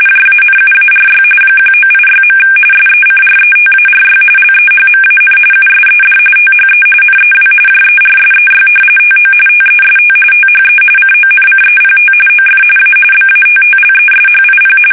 RTTY, about 45.45 baud. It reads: JANUARY 1986. SECRETARY LEHMAN USED THE NEW JERSEY TO TEST THE CONCEPT OF A BATTLESHIP
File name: fsk850 / battleshipnj